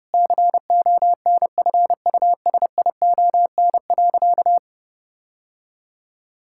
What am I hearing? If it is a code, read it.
Morse code, heard as a keyed tone, about 30 wpm, 700 Hz. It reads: CONFUSION.
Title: Sentences from Top 1600 Words 30wpm